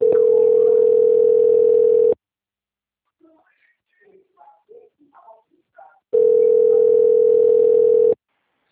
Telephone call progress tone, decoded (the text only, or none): ringback tone